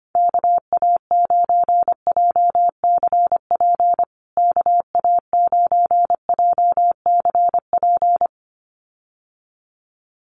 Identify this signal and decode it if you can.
Morse code, heard as a keyed tone, about 25 wpm, 700 Hz. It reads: KA9JCP KA9JCP
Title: USA Call Signs Repeated 2x 25wpm